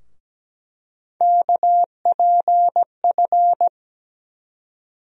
Morse code, heard as a keyed tone, about 17 wpm, 700 Hz. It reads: KPF
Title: USA Call Sign Postfixes - 3 Letters 17wpm